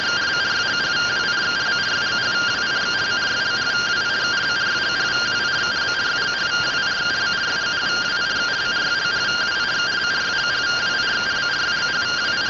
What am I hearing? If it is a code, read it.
Two-tone teleprinter signal, about 170 Hz shift.